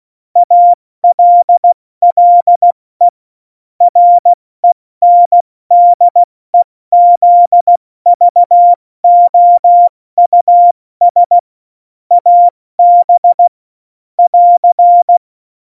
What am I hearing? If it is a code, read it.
Morse code, heard as a keyed tone, about 16 wpm, 700 Hz. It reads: ALLE RENDEZVOUS AB <AR>
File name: Morsekurs / Witze / Witz 07 April 2020, 70 F80